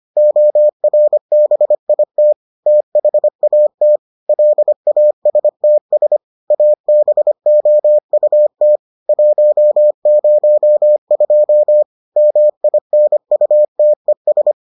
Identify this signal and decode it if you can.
Morse code, heard as a keyed tone, about 25 wpm, 600 Hz. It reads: ORBIT THAT LASTS ABOUT 102 MINUTES